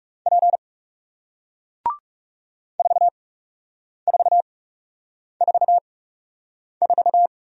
Morse code, heard as a keyed tone, about 40 wpm, spaced as for 8 wpm, 700 Hz. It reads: P 4444